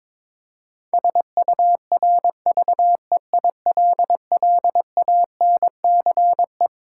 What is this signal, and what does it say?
Morse code, heard as a keyed tone, about 22 wpm, 700 Hz. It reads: SURVEILLANCE
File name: Twelve Letter Words - 22wpm